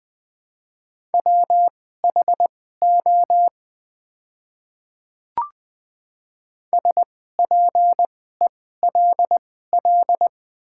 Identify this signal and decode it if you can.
Morse code, heard as a keyed tone, about 20 wpm, spaced as for 14 wpm, 700 Hz. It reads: WHO SPELL
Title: 2x Character Spacing - Top 200 Words - 20wpm